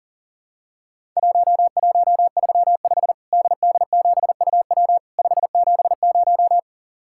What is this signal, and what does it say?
Morse code, heard as a keyed tone, about 37 wpm, 700 Hz. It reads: 1135 DD7UW 570
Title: Worked All Europe DX Contest - QTC Exchange - 40wpm